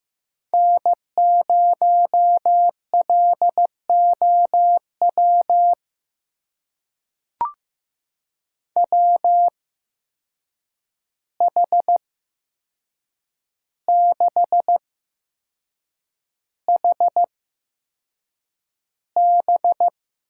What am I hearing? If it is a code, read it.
Morse code, heard as a keyed tone, about 15 wpm, spaced as for 6 wpm, 700 Hz. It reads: N0LOW WH6HB
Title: USA Call Signs - 5 Characters - 8x Character Spacing - 15wpm